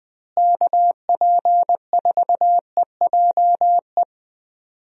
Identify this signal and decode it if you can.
Morse code, heard as a keyed tone, about 20 wpm, 700 Hz. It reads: KP4EJE
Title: DXCC-100 Call Signs - 6 Characters - Repeated 3x - 20wpm